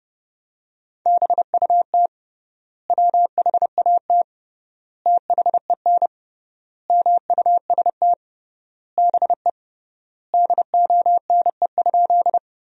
Morse code, heard as a keyed tone, about 30 wpm, 700 Hz. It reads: BUT WHAT THEN MUST BE DONE?
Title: 3x Word-Spacing - Sentences from Top 500 Words - 30wpm